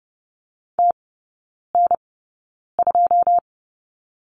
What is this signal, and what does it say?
Morse code, heard as a keyed tone, about 30 wpm, spaced as for 9 wpm, 700 Hz. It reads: TN2
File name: 081 Lesson 14e - Number 2 - Three Characters 30wpm